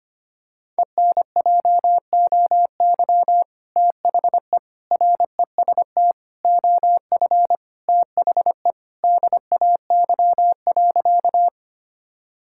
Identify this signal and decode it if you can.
Morse code, heard as a keyed tone, about 25 wpm, 700 Hz. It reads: ENJOY THE REST OF THE DAY.
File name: practice / 25wpm / Sentences with 2-Word Phrases - 25wpm